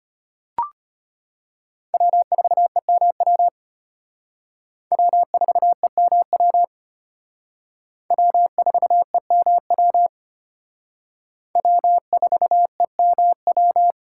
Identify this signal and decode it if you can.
Morse code, heard as a keyed tone, about 31 wpm, 700 Hz. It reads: W4EMW W4EMW W4EMW W4EMW
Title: Speed Racing - USA 1x3 Call Signs 25wpm